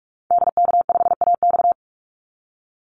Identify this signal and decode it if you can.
Morse code, heard as a keyed tone, about 44 wpm, 700 Hz. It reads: DK5AX
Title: Call Signs Repeated 3x - 45wpm